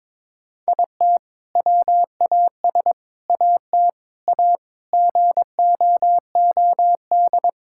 Morse code, heard as a keyed tone, about 22 wpm, 700 Hz. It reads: IT WAS AT A GOOD
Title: Sentences from Top 1000 Words - Review Part 5 - 22wpm